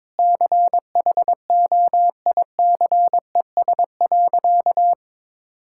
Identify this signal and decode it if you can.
Morse code, heard as a keyed tone, about 22 wpm, 700 Hz. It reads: CHOICES.
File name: practice / 22wpm / Sentences with 3-Word Phrases - 22wpm